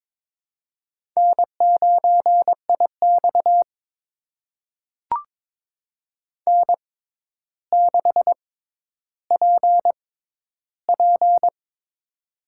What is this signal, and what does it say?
Morse code, heard as a keyed tone, about 23 wpm, spaced as for 11 wpm, 700 Hz. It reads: N9IX N6PP
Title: USA Call Signs - 4 Characters - 6x Character Spacing - 22wpm